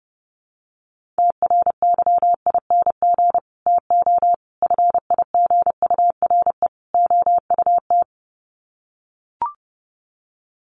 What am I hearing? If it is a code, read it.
Morse code, heard as a keyed tone, about 30 wpm, 700 Hz. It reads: TRYING TO FIGURE OUT
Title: Sets of 4 Words 30wpm